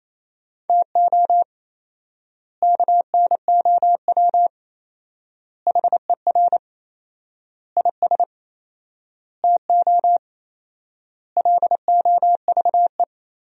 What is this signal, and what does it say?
Morse code, heard as a keyed tone, about 28 wpm, 700 Hz. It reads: TO KNOW HER IS TO LOVE
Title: 4x Word-Spacing - Sentences from Top 400 Words - 28wpm